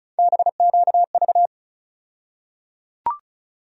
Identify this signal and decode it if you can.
Morse code, heard as a keyed tone, about 36 wpm, 700 Hz. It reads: BQV
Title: USA Call Sign Postfixes - 3 Letters 35wpm